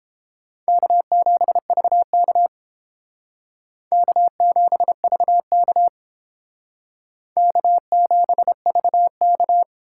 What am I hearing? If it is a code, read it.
Morse code, heard as a keyed tone, about 30 wpm, 700 Hz. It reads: K7VK K7VK K7VK